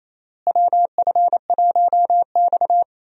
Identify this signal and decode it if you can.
Morse code, heard as a keyed tone, about 28 wpm, 700 Hz. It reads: WF1X
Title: USA Call Signs - 4 Characters - Repeated 2x - 28wpm